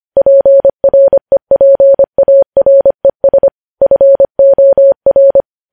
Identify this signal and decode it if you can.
Morse code, heard as a keyed tone, about 25 wpm, 550 Hz. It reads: PREPARES FOR
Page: News Headlines in Morse Code at 25 WPM